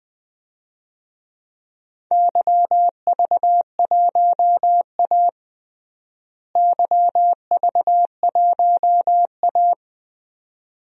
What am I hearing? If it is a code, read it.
Morse code, heard as a keyed tone, about 20 wpm, 700 Hz. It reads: YV1A YV1A
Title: DXCC-100 Call Signs - 4 Characters - Repeated 2x - 20wpm